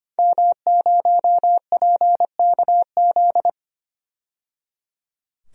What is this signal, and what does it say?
Morse code, heard as a keyed tone, about 25 wpm, 700 Hz. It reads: M0PKZ
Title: DXCC-100 Call Signs - 5 Characters - 25wpm